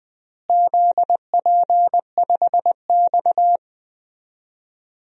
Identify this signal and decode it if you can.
Morse code, heard as a keyed tone, about 20 wpm, 700 Hz. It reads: ZP5X
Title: DXCC-100 Call Signs - 4 Characters - Repeated 2x - 20wpm